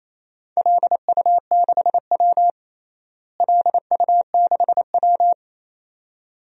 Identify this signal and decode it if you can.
Morse code, heard as a keyed tone, about 28 wpm, 700 Hz. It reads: LU6W LU6W
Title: DXCC-100 Call Signs - 4 Characters - Repeated 2x - 28wpm